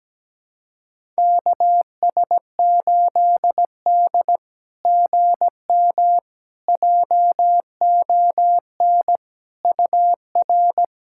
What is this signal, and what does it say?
Morse code, heard as a keyed tone, about 17 wpm, 690 Hz. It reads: KS8D GM JON UR
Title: SOTA QSO Practice 17wpm